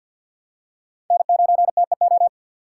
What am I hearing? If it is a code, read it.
Morse code, heard as a keyed tone, about 47 wpm, 700 Hz. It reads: N9TEO
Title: USA Call Signs - 4x Character Spacing - 50wpm